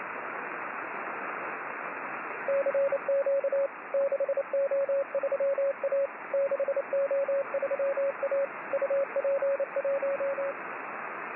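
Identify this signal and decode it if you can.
Morse code, heard as a keyed tone, about 28 wpm, 550 Hz, with radio noise. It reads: CQ 6O3A 6O3A UP1
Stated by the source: Morse signal on the 6 m band.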